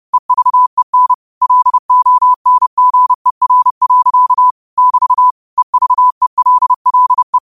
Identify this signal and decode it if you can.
Morse code, heard as a keyed tone, about 30 wpm, 1000 Hz. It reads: EVEN LONGER. <BT> EVELLE
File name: qotdfiles / 30wpm